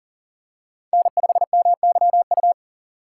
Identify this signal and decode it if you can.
Morse code, heard as a keyed tone, about 37 wpm, 700 Hz. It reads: N5MYU
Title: USA Call Signs - 5 Characters - 6x Character Spacing - 40wpm